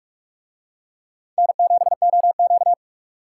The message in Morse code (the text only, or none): N7OQ